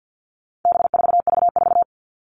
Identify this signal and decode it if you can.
Morse code, heard as a keyed tone, about 50 wpm, 700 Hz. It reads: B4V4
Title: All Cut Numbers - 4-Digit 50wpm